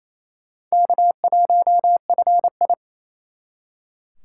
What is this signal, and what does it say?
Morse code, heard as a keyed tone, about 38 wpm, spaced as for 33 wpm, 700 Hz. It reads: K1FI
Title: Speed Racing - USA 1x2 Call Signs 28wpm